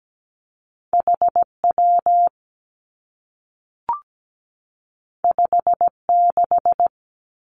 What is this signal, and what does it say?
Morse code, heard as a keyed tone, about 17 wpm, 700 Hz. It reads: H W 5 6
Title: Two Challenging Letter-Number Combinations 17wpm